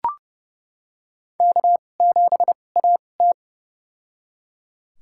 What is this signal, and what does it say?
Morse code, heard as a keyed tone, about 30 wpm, spaced as for 22 wpm, 700 Hz. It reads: K7AT
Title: USA Call Signs - 4 Characters - 2x Character Spacing - 30wpm